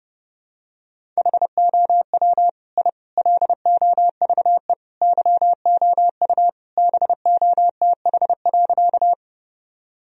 Morse code, heard as a keyed tone, about 32 wpm, 700 Hz. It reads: HOW I LOVE YOU BOTH.
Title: Sentences from Top 1000 Words - Review Part 1 - 30wpm